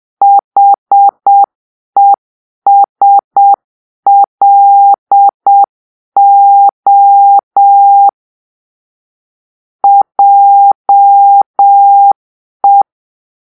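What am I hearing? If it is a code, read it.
Morse code, heard as a keyed tone, about 7 wpm, 800 Hz. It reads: HESLO JE
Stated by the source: Morse-code cipher made by a player team for the Kachna puzzle game.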